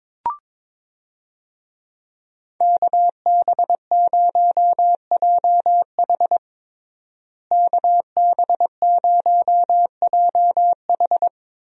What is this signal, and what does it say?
Morse code, heard as a keyed tone, about 22 wpm, 700 Hz. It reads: KB0JH KB0JH
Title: USA Call Signs - 5 Characters - Repeated 3x - 22wpm